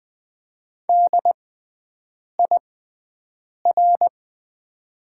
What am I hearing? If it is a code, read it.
Morse code, heard as a keyed tone, about 20 wpm, spaced as for 7 wpm, 700 Hz. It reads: DIR
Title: 6x Character Spacing - Top 500 Words - 20wpm